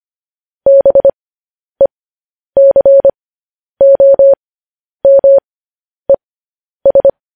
Morse code, heard as a keyed tone, about 25 wpm, spaced as for 10 wpm, 550 Hz. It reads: BECOMES
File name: podcast / News headlines-1x-25x10